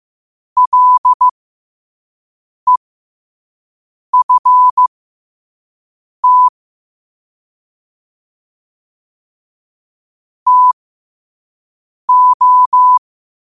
Morse code, heard as a keyed tone, about 15 wpm, spaced as for 5 wpm, 1000 Hz. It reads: LEFT TO
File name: 2026-02-15-5wpm